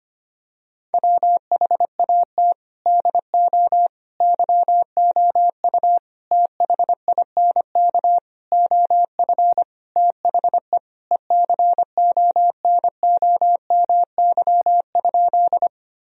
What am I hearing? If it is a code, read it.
Morse code, heard as a keyed tone, about 25 wpm, 700 Hz. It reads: WHAT DO YOU THINK OF THE ECONOMY?
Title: Sentences from Top 1000 Words - Review Part 3 - 25wpm